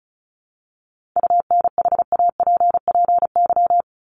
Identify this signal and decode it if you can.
Morse code, heard as a keyed tone, about 34 wpm, 700 Hz. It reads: UNHAPPY
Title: Top 1501-1750 Words 35wpm